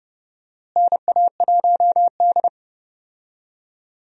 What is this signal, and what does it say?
Morse code, heard as a keyed tone, about 30 wpm, 700 Hz. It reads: NA1D
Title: USA Call Signs - 4 Characters - Repeated 3x - 30wpm